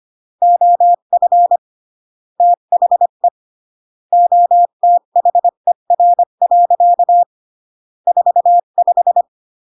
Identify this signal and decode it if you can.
Morse code, heard as a keyed tone, about 25 wpm, 700 Hz. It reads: OF THE OTHER. 45
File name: CW / jodorowsky / 82 commandments 25-20